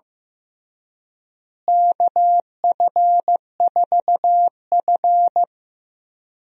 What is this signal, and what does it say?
Morse code, heard as a keyed tone, about 15 wpm, 700 Hz. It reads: KF4F
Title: USA Call Signs - 4 Characters - Repeated 3x - 15wpm